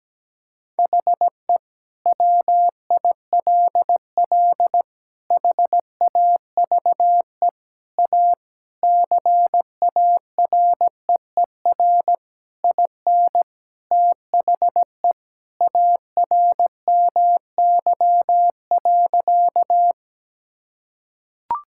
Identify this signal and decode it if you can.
Morse code, heard as a keyed tone, about 17 wpm, 700 Hz. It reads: HE WILL HAVE A CAREER IN THE ARMY.
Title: Sentences from Top 1000 Words - Review Part 6 - 17wpm